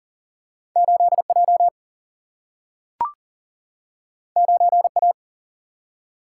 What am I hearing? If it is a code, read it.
Morse code, heard as a keyed tone, about 39 wpm, 700 Hz. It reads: 8J 9A
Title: Single-Letter Single-Number 40wpm